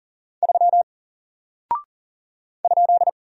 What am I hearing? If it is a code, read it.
Morse code, heard as a keyed tone, about 40 wpm, 700 Hz. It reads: 3 ?
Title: Single Character (Letter:Number:Punctuation:Prosign) Alt 40wpm